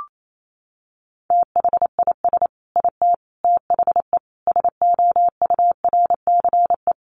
Morse code, heard as a keyed tone, about 28 wpm, 700 Hz. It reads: THIS IT THE SOURCE
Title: Sentences from Top 800 Words 28wpm